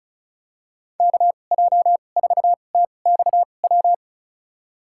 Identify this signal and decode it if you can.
Morse code, heard as a keyed tone, about 36 wpm, spaced as for 25 wpm, 700 Hz. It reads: KJ4TXW
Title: USA Call Signs - 6 Characters - 2x Character Spacing - 35wpm